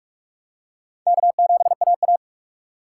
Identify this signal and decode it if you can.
Morse code, heard as a keyed tone, about 44 wpm, 700 Hz. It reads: K7AA